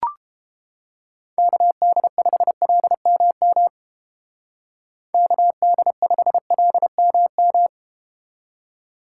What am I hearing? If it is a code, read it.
Morse code, heard as a keyed tone, about 30 wpm, 700 Hz. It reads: KD5LMM KD5LMM